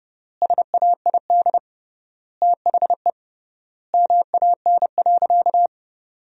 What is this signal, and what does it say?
Morse code, heard as a keyed tone, about 30 wpm, 700 Hz. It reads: SAID THE MAN.
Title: 3x Word-Spacing - Sentences from Top 400 Words - 30wpm